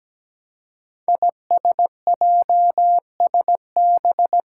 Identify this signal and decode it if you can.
Morse code, heard as a keyed tone, about 17 wpm, 700 Hz. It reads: ISJSB
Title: Five Challenging Letter-Number Combinations 17wpm